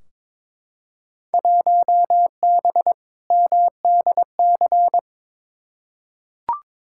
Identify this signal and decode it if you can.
Morse code, heard as a keyed tone, about 22 wpm, 700 Hz. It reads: 1B MDC
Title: Field Day Exchanges Encore - 22wpm